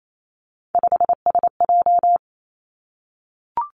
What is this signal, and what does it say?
Morse code, heard as a keyed tone, about 28 wpm, 700 Hz. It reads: 5SJ
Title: Three Challenging Letter-Number Combinations 28wpm